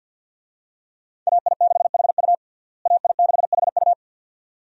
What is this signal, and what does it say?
Morse code, heard as a keyed tone, about 47 wpm, 700 Hz. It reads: AI6HU AI6HU
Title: USA Call Signs - 5 Characters - Repeated 2x - 50wpm